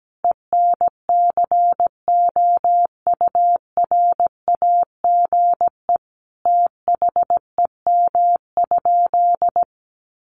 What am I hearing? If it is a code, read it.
Morse code, heard as a keyed tone, about 17 wpm, 700 Hz. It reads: ENCOURAGE THEM?
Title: Sentences from Top 1400 Words 17wpm